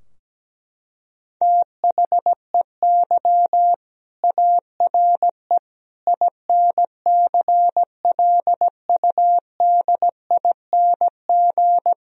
Morse code, heard as a keyed tone, about 17 wpm, 700 Hz. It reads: THEY ARE INCLUDING
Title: Sentences from Top 1000 Words - Review Part 4 - 17wpm